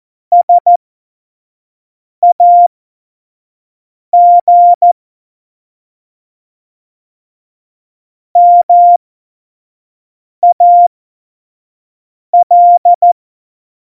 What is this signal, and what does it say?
Morse code, heard as a keyed tone, about 14 wpm, spaced as for 5 wpm, 700 Hz. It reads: SAG MAL
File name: Morsekurs / Witze / Witz 27. Februar 2018, 25 F70